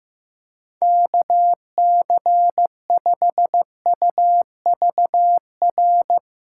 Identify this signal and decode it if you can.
Morse code, heard as a keyed tone, about 15 wpm, 700 Hz. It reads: KC5UVR
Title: USA Callsign DE Callsign 15wpm